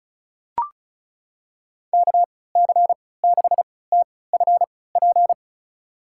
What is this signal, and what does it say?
Morse code, heard as a keyed tone, about 36 wpm, spaced as for 20 wpm, 700 Hz. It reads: KC6TFP
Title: USA Call Signs - 6 Characters - 3x Character Spacing - 35wpm